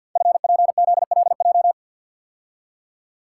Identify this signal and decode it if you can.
Morse code, heard as a keyed tone, about 49 wpm, 700 Hz. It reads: FPZLJ
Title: Code Groups - 5 Letter:Numbers - 50wpm